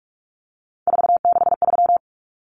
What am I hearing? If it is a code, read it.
Morse code, heard as a keyed tone, about 44 wpm, 700 Hz. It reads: 463
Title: Common Cut Numbers - 3-Digit 45wpm